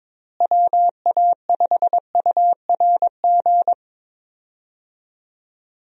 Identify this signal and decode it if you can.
Morse code, heard as a keyed tone, about 22 wpm, 700 Hz. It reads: WA5URG
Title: USA Call Signs - 6 Characters - Repeated 2x - 22wpm